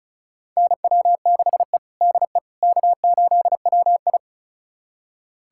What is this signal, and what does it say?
Morse code, heard as a keyed tone, about 36 wpm, 700 Hz. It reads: NW6E DE K8WI
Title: USA Callsign DE Callsign - 4 Characters - 35wpm